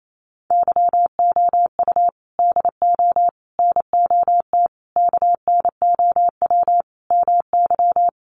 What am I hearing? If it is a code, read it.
Morse code, heard as a keyed tone, about 28 wpm, 700 Hz. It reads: YOU DO NOT KNOW MY
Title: Sentences from Top 600 Words 28wpm